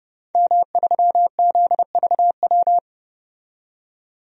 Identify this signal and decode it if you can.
Morse code, heard as a keyed tone, about 31 wpm, 700 Hz. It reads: M3ZVW
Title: DXCC-100 Call Signs - 5 Characters - 30wpm